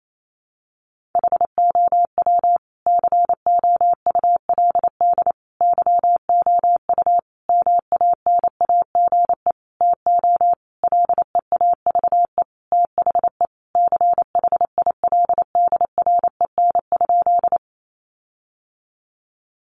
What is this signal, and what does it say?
Morse code, heard as a keyed tone, about 28 wpm, 700 Hz. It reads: HOW COULD YOU MANAGE TO LEAVE THE CHILDREN?
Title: Sentences from Top 1400 Words 28wpm